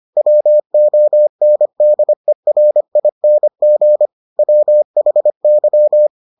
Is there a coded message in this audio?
Yes: Morse code from 0.2 to 6.1 s, heard as a keyed tone, about 25 wpm, 600 Hz. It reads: WONDERING WHY